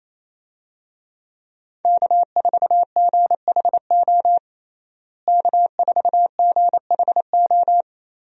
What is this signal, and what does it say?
Morse code, heard as a keyed tone, about 28 wpm, 700 Hz. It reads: K4GHO K4GHO
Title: USA Call Signs - 5 Characters - Repeated 2x - 28wpm